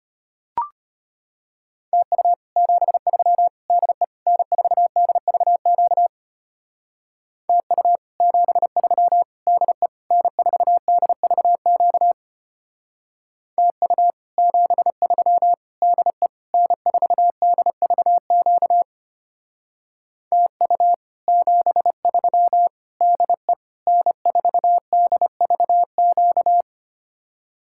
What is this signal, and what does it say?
Morse code, heard as a keyed tone, about 32 wpm, 700 Hz. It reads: TU 73 DE N4DVQ TU 73 DE N4DVQ TU 73 DE N4DVQ TU 73 DE N4DVQ